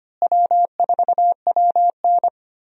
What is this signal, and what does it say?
Morse code, heard as a keyed tone, about 25 wpm, 700 Hz. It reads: W4WN
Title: USA Call Signs - 4 Characters - Repeated 2x - 25wpm